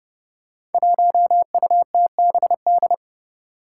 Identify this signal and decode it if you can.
Morse code, heard as a keyed tone, about 30 wpm, 700 Hz. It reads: 1UTBD
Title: Five Challenging Letter-Number Combinations 30wpm